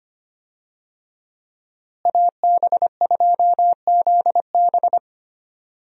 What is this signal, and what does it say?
Morse code, heard as a keyed tone, about 25 wpm, 700 Hz. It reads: AB2ZB
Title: USA Call Signs - 5 Characters - Repeated 3x - 25wpm